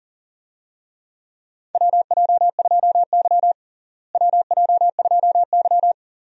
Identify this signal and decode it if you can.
Morse code, heard as a keyed tone, about 40 wpm, 700 Hz. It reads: WJ2Y WJ2Y
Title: USA Call Signs - 4 Characters - Repeated 2x - 40wpm